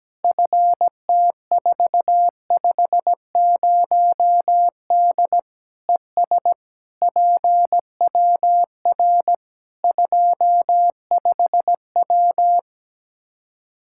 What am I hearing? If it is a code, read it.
Morse code, heard as a keyed tone, about 17 wpm, 700 Hz. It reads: FT450D ES PWR 25W
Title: K1USN SST QSO Practice 17wpm